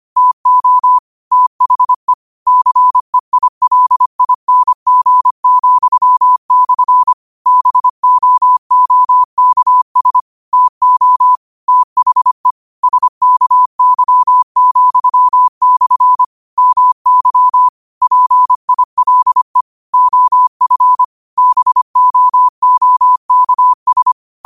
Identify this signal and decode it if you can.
Morse code, heard as a keyed tone, about 25 wpm, 1000 Hz. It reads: TO THE CEILING,/ BOOKS TO THE SKY,/ MY PILE OF BOOKS
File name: qotdfiles / 25wpm